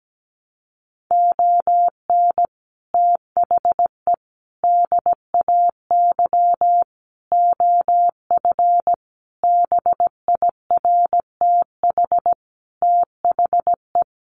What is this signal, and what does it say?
Morse code, heard as a keyed tone, about 17 wpm, 700 Hz. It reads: ON THE DAY OF BIRTH THE
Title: Sentences from Top 1100 Words 17wpm